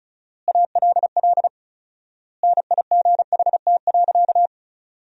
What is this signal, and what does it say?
Morse code, heard as a keyed tone, about 36 wpm, 700 Hz. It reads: ALL NIGHT.
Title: 4x Word-Spacing - Sentences from Top 300 Words - 35wpm